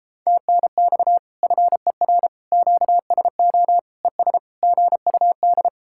Morse code, heard as a keyed tone, about 31 wpm, 700 Hz. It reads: TNX FER QSO ES GUD